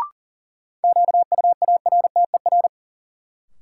Morse code, heard as a keyed tone, about 40 wpm, 700 Hz. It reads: QUARTER
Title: Sentences from Top 1300 Words - New Word Drill Alt - 40wpm